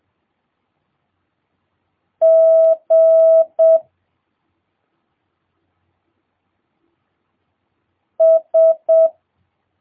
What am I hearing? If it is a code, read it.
Morse code, heard as a keyed tone, about 7 wpm, 650 Hz. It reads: G S
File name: jh8cbh / 2022cw